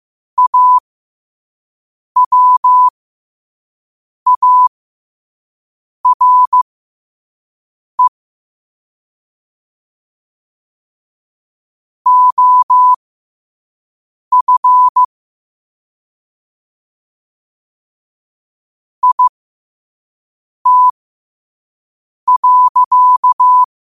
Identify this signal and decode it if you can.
Morse code, heard as a keyed tone, about 15 wpm, spaced as for 5 wpm, 1000 Hz. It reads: AWARE OF IT.
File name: qotdfiles / 5wpm